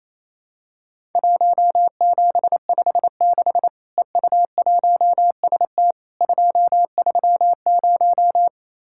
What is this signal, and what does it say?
Morse code, heard as a keyed tone, about 28 wpm, 700 Hz. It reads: 1756 EU1ST 230
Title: Worked All Europe DX Contest - QTC Exchange - 28wpm